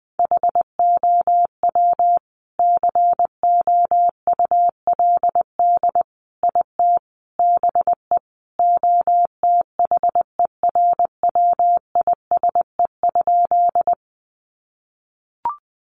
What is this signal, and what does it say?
Morse code, heard as a keyed tone, about 20 wpm, 700 Hz. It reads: HOW COULD IT BE OTHERWISE?
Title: Sentences from Top 1200 Words 20wpm